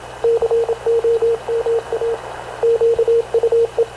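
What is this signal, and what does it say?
Morse code, heard as a keyed tone, about 26 wpm, 460 Hz, with radio noise. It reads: COMA QUE
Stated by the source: Havana Radio / CLA sending Spanish press (PX) in Morse, recorded by Radio Officer Paul Dumesnil onboard CNAV QUEST/CZDO.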